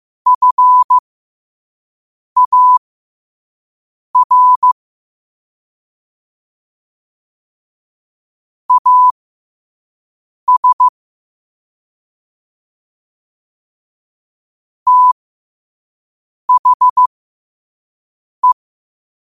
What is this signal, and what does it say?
Morse code, heard as a keyed tone, about 15 wpm, spaced as for 5 wpm, 1000 Hz. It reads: FAR AS THE
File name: qotdfiles / 5wpm